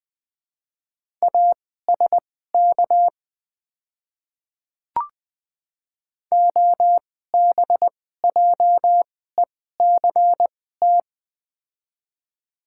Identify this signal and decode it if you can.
Morse code, heard as a keyed tone, about 20 wpm, spaced as for 14 wpm, 700 Hz. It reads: ASK OBJECT
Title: 2x Character Spacing - Top 500 Words - 20wpm